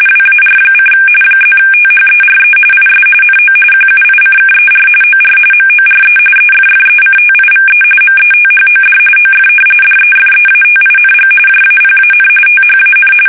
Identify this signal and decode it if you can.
RTTY, about 45.45 baud. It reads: WAR: +THERE IS NO QUICKER OR MORE EFFECTIVE METHOD OF DEFEATING JAPAN THAN
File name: fsk850